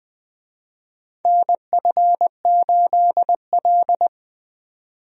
Morse code, heard as a keyed tone, about 20 wpm, 700 Hz. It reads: NF8L
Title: USA Call Signs - 4 Characters - 20wpm